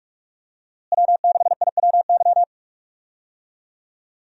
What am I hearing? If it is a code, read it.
Morse code, heard as a keyed tone, about 47 wpm, 700 Hz. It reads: W6IWY
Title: USA Call Signs - 5 Characters - Repeated 2x - 45wpm